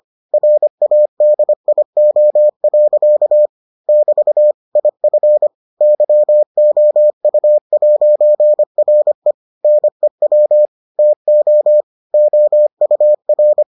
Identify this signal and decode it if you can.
Morse code, heard as a keyed tone, about 25 wpm, 600 Hz. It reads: RADIO. <BT> IF YOU'RE NEW TO OUR